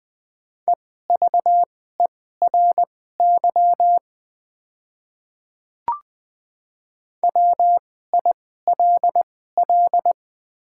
Morse code, heard as a keyed tone, about 20 wpm, spaced as for 15 wpm, 700 Hz. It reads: EVERY WILL
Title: 2x Character Spacing - Top 200 Words - 20wpm